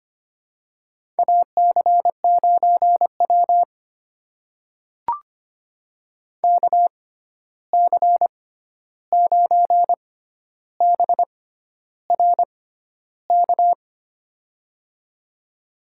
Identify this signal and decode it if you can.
Morse code, heard as a keyed tone, about 25 wpm, spaced as for 11 wpm, 700 Hz. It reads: AC9W KC9BRK
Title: USA Call Signs - 6x Character Spacing - 25wpm